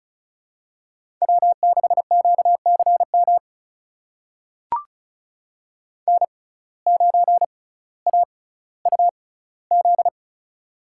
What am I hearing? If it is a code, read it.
Morse code, heard as a keyed tone, about 36 wpm, spaced as for 18 wpm, 700 Hz. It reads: W6QCM N9AUZ
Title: USA Call Signs - 5 Characters - 6x Character Spacing - 35wpm